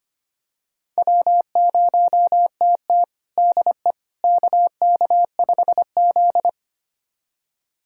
Morse code, heard as a keyed tone, about 25 wpm, 700 Hz. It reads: W0TT DE KK5Z
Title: USA Callsign DE Callsign - 4 Characters - 25wpm